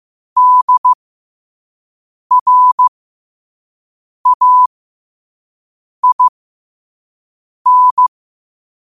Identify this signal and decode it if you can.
Morse code, heard as a keyed tone, about 15 wpm, spaced as for 5 wpm, 1000 Hz. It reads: DRAIN
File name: qotdfiles / 5wpm